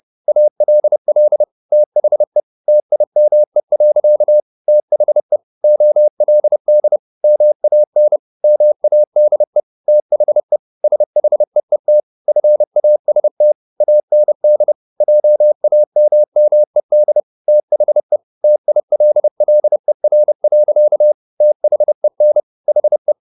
Morse code, heard as a keyed tone, about 30 wpm, 600 Hz. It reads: ALL THE TIME. THE OLD MAN MADE THE SHEET FAST AND JAMMED THE TILLER. THEN HE